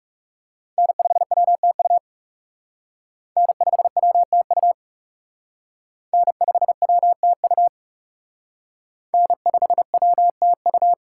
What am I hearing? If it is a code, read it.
Morse code, heard as a keyed tone, about 37 wpm, 700 Hz. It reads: N5WTU N5WTU N5WTU N5WTU
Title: Speed Racing - USA 1x3 Call Signs 30wpm